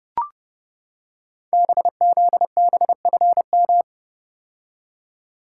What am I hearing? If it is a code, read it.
Morse code, heard as a keyed tone, about 29 wpm, 700 Hz. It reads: BZBFM
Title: Code Groups - 5 Letters - 30wpm